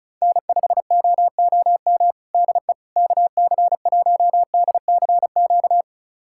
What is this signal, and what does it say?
Morse code, heard as a keyed tone, about 34 wpm, 690 Hz. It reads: N5OOM DE KC1DCQ
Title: USA Callsign DE Callsign 35wpm